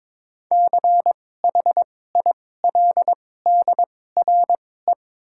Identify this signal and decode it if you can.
Morse code, heard as a keyed tone, about 21 wpm, spaced as for 16 wpm, 700 Hz. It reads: CHILDRE
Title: 2x Character Spacing - Top 500 Words - 22wpm